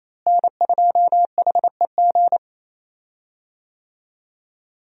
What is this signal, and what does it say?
Morse code, heard as a keyed tone, about 28 wpm, 700 Hz. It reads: N2HEG
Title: USA Call Signs - 5 Characters - Repeated 3x - 28wpm